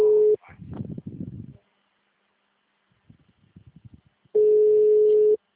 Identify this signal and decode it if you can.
Ringback tone (Europe, 425 Hz).
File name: out- -2003- - - .0